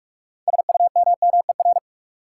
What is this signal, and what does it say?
Morse code, heard as a keyed tone, about 46 wpm, 700 Hz. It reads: SUMMER
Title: Sentences from Top 1000 Words - Review Part 6 - New Word Drill - 45wpm